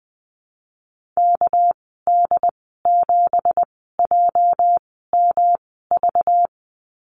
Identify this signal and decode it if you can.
Morse code, heard as a keyed tone, about 20 wpm, spaced as for 15 wpm, 700 Hz. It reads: KD7JMV
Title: 219 Lesson 35g - Number 0 - Call Signs 20wpm